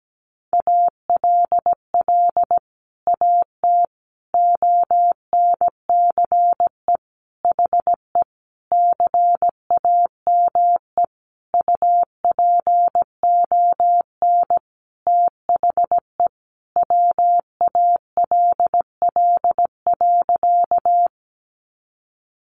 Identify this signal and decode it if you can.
Morse code, heard as a keyed tone, about 17 wpm, 700 Hz. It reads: ALL AT ONCE HE CAME UPON THE WALL.
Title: Sentences from Top 1100 Words 17wpm